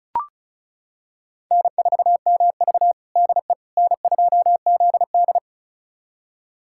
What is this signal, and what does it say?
Morse code, heard as a keyed tone, about 36 wpm, 700 Hz. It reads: N4MV DE N2ZD
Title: USA Callsign DE Callsign - 4 Characters - 35wpm